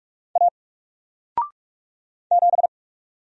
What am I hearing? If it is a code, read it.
Morse code, heard as a keyed tone, about 44 wpm, 700 Hz. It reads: A 7
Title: Single Character (Letter:Number:Punctuation:Prosign) Alt 45wpm